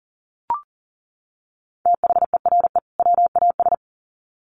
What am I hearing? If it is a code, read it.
Morse code, heard as a keyed tone, about 40 wpm, 700 Hz. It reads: THERE WAS
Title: Sets of 2 Words 40wpm